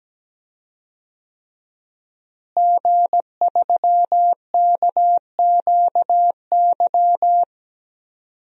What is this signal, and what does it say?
Morse code, heard as a keyed tone, about 17 wpm, 700 Hz. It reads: G3KQY
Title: DXCC-100 Call Signs - 5 Characters - Repeated 3x - 17wpm